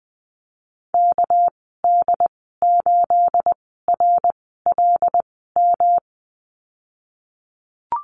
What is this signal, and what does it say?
Morse code, heard as a keyed tone, about 19 wpm, spaced as for 14 wpm, 700 Hz. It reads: KD8RLM
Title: 231 Lesson 37g - Letter x - Call Signs 20wpm